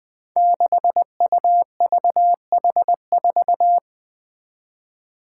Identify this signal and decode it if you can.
Morse code, heard as a keyed tone, about 20 wpm, 700 Hz. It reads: 6UVH4
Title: Five Challenging Letter-Number Combinations 20wpm